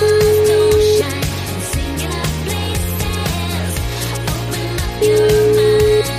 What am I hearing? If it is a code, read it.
Ringback tone (Europe, 425 Hz), with noise.